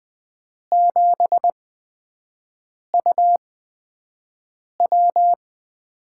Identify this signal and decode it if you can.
Morse code, heard as a keyed tone, about 20 wpm, spaced as for 5 wpm, 700 Hz. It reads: 7UW
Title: USA Call Signs - 4 Characters - 8x Character Spacing - 20wpm